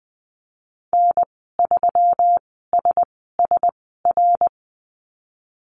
Morse code, heard as a keyed tone, about 19 wpm, spaced as for 14 wpm, 700 Hz. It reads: N3SSR
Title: 128 Lesson 21g - Number 6 - Call Signs 20wpm